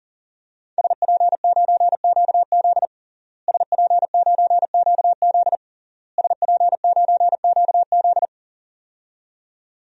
Morse code, heard as a keyed tone, about 40 wpm, 700 Hz. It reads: SP9QZ SP9QZ SP9QZ
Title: DXCC-100 Call Signs - 5 Characters - Repeated 3x - 40wpm